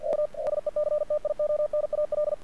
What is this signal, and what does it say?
Morse code, heard as a keyed tone, about 50 wpm, 600 Hz. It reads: U SECTIONAL
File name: archived / CRSnet